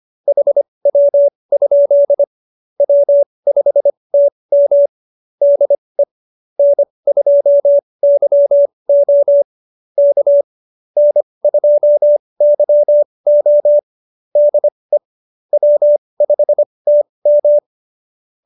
Morse code, heard as a keyed tone, about 25 wpm, spaced as for 20 wpm, 570 Hz. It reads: HW? W5TM DE N2YO K N2YO DE W5TM
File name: QSO316 20